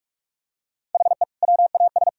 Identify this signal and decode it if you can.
Morse code, heard as a keyed tone, about 46 wpm, 700 Hz. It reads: HE WAS
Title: Sentences from Top 1000 Words - Review Part 1 - 45wpm